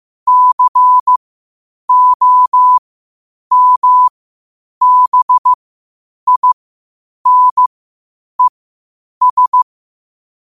Morse code, heard as a keyed tone, about 15 wpm, spaced as for 8 wpm, 1000 Hz. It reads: COMBINES
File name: qotdfiles / 8wpm